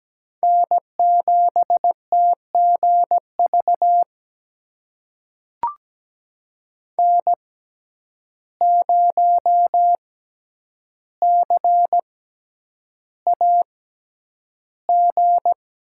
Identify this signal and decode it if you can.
Morse code, heard as a keyed tone, about 17 wpm, spaced as for 9 wpm, 700 Hz. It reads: N7TGV N0CAG
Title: USA Call Signs - 5 Characters - 6x Character Spacing - 17wpm